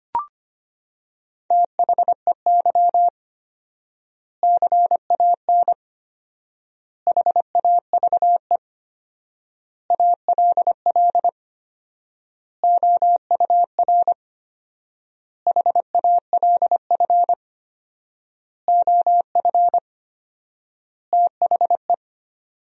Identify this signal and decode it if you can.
Morse code, heard as a keyed tone, about 25 wpm, 700 Hz. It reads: THEY CAN HAVE ALL OUR HALF OF THE
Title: 4x Word-Spacing - Sentences from Top 400 Words - 25wpm